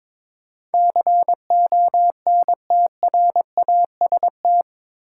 Morse code, heard as a keyed tone, about 22 wpm, 700 Hz. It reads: CONTRAST
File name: practice / 22wpm / Eight Letter Words - 22wpm